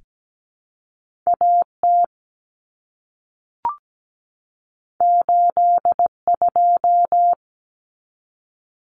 Morse code, heard as a keyed tone, about 17 wpm, 700 Hz. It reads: A T 8 2
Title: Common Cut Numbers - 2-Digit 17wpm